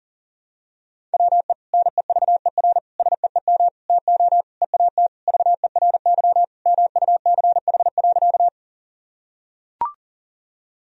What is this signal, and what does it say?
Morse code, heard as a keyed tone, about 37 wpm, 700 Hz. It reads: WE NEVER SEEM TO EAT VERY MUCH.
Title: Sentences from Top 1000 Words - Review Part 1 - 40wpm